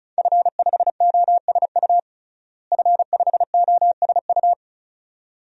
Morse code, heard as a keyed tone, about 34 wpm, 700 Hz. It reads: F5OSU F5OSU
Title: DXCC-100 Call Signs - 5 Characters - Repeated 2x - 35wpm